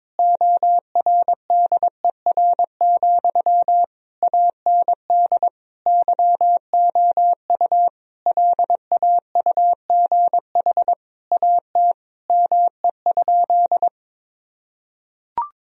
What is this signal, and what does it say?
Morse code, heard as a keyed tone, about 22 wpm, 700 Hz. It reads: ORDER, AND YOU LAUGH AT ME?
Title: Sentences from Top 1000 Words - Review Part 2 - 22wpm